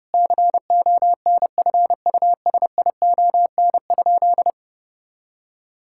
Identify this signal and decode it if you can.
Morse code, heard as a keyed tone, about 30 wpm, 700 Hz. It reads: CONFUSION?
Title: Sentences from Top 1600 Words 30wpm